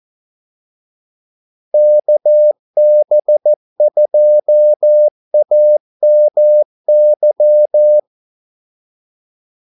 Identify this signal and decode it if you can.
Morse code, heard as a keyed tone, about 14 wpm, 600 Hz. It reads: KB2AMY
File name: Callsigns-14wpm